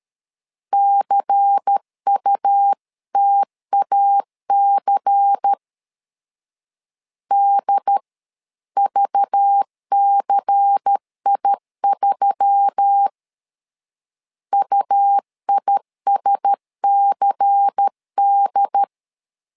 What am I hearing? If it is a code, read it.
Morse code, heard as a keyed tone, about 13 wpm, 800 Hz. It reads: CUTAC DVCI3 UISCD
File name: LEZIONI / 5 LBVC / 5 LBVC 12wpm 1955